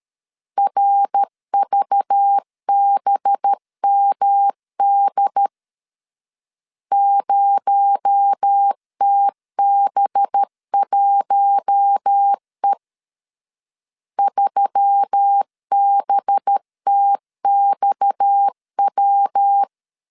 Morse code, heard as a keyed tone, about 13 wpm, 800 Hz. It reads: RVBMD 0TB1E 3BTXW
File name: LEZIONI / 7 GF9Z / 7 GF9 12wpm 2319